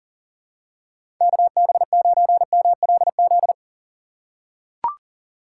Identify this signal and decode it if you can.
Morse code, heard as a keyed tone, about 40 wpm, 700 Hz. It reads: KB9MLZ
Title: USA Call Signs - 6 Characters - 4x Character Spacing - 40wpm